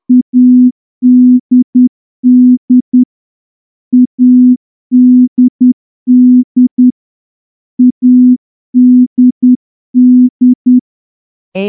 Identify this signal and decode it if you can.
Morse code, heard as a keyed tone, about 10 wpm, 250 Hz. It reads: ADD ADD ADD
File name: morse-training-10WPM